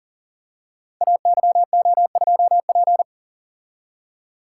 Morse code, heard as a keyed tone, about 42 wpm, 690 Hz. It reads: AYO2P
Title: Code Groups - 5 Letter:Numbers - 40wpm